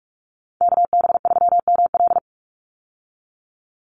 Morse code, heard as a keyed tone, about 44 wpm, 700 Hz. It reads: KB3ML
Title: Call Signs - 45wpm